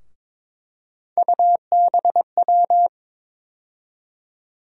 Morse code, heard as a keyed tone, about 22 wpm, 700 Hz. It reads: UBW
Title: USA Call Sign Postfixes - 3 Letters 22wpm